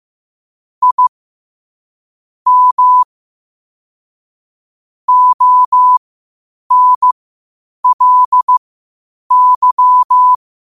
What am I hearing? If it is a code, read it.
Morse code, heard as a keyed tone, about 15 wpm, spaced as for 8 wpm, 1000 Hz. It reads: I M ONLY
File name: qotdfiles / 8wpm